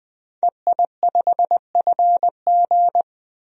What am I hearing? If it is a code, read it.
Morse code, heard as a keyed tone, about 20 wpm, 700 Hz. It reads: EI5FG
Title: DXCC-100 Call Signs - 5 Characters - 20wpm